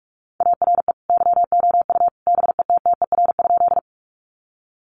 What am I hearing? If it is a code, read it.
Morse code, heard as a keyed tone, about 44 wpm, 700 Hz. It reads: ARE YOU BETTER?
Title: Sentences from Top 400 Words 45wpm